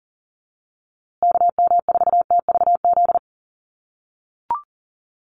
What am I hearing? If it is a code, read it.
Morse code, heard as a keyed tone, about 40 wpm, 700 Hz. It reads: KM4TVZ
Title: USA Call Signs Repeated 3x 40wpm